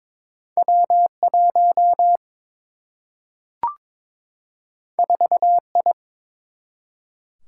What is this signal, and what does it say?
Morse code, heard as a keyed tone, about 23 wpm, 700 Hz. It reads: W1 4I
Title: Single-Letter Single-Number 22wpm